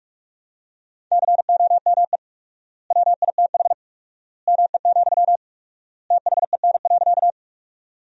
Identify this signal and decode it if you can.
Morse code, heard as a keyed tone, about 42 wpm, 700 Hz. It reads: COME WITH ME, THEN.
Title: 4x Word-Spacing - Sentences from Top 200 Words - 45wpm